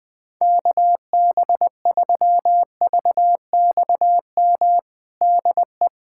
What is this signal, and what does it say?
Morse code, heard as a keyed tone, about 21 wpm, 700 Hz. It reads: KB3VXM DE
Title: USA Callsign DE Callsign 20wpm